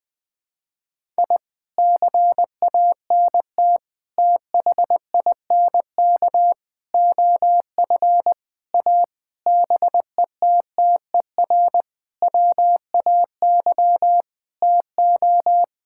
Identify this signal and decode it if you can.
Morse code, heard as a keyed tone, about 20 wpm, 700 Hz. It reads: I CANT THINK OF A BETTER WAY TO
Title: Sentences with 8-Word Phrases - 20wpm